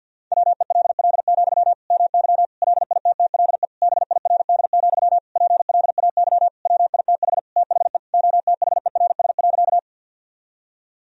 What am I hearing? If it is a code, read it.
Morse code, heard as a keyed tone, about 51 wpm, 700 Hz. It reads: WELL, MY LITTLE BIRD, PLAY WITH THE OTHERS.
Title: Sentences from Top 1000 Words - Review Part 2 - 50wpm